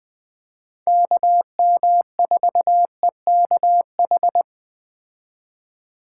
Morse code, heard as a keyed tone, about 20 wpm, 700 Hz. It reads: KM4EKH
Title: Speed Racing - USA Call Signs 17wpm